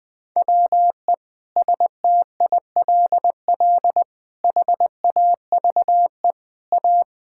Morse code, heard as a keyed tone, about 20 wpm, 700 Hz. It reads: WE STILL HAVE A
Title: Sentences with 9-Word Phrases - 20wpm